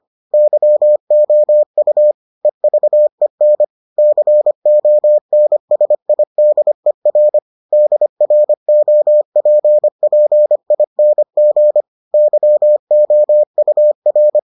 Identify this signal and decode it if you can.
Morse code, heard as a keyed tone, about 25 wpm, 600 Hz. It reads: YOU EVEN CONSIDER DROPPING YOUR